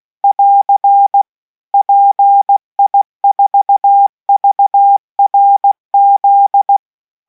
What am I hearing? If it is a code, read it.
Morse code, heard as a keyed tone, about 16 wpm, 800 Hz. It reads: <AR> PI4VRZ